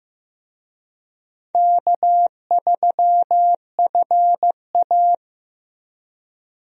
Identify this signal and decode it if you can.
Morse code, heard as a keyed tone, about 16 wpm, 700 Hz. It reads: K3FA
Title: USA Call Signs - 4 Characters - Repeated 3x - 15wpm